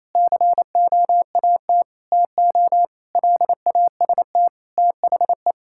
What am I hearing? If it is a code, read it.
Morse code, heard as a keyed tone, about 28 wpm, 700 Hz. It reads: COAT TO LAST THE
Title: Sentences from Top 1600 Words 28wpm